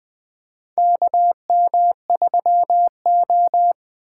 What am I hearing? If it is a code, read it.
Morse code, heard as a keyed tone, about 20 wpm, 700 Hz. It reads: KM3O
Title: USA Call Signs - 8x Character Spacing - 20wpm